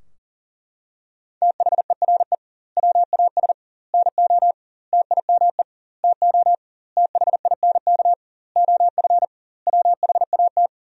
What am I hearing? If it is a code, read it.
Morse code, heard as a keyed tone, about 37 wpm, 700 Hz. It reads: THERE WAS NO TIME TO THINK OF WHAT
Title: 2x Word-Spacing - Sentences from Top 400 Words - 40wpm